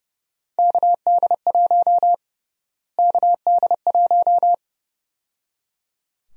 Morse code, heard as a keyed tone, about 30 wpm, 700 Hz. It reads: KD1 KD1
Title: USA Call Sign Prefixes - Repeated 2x - 30wpm